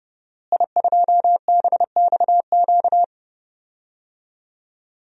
Morse code, heard as a keyed tone, about 30 wpm, 700 Hz. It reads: I2BXQ
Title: DXCC-100 Call Signs - 5 Characters - Repeated 3x - 30wpm